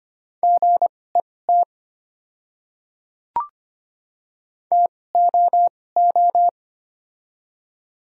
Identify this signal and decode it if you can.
Morse code, heard as a keyed tone, about 25 wpm, spaced as for 18 wpm, 700 Hz. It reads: GET TOO
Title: 2x Character Spacing - Top 300 Words - 25wpm